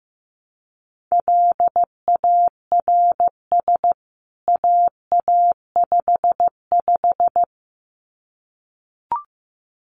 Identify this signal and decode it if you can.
Morse code, heard as a keyed tone, about 15 wpm, 700 Hz. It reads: LARS AA55
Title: CWT Exchange 15wpm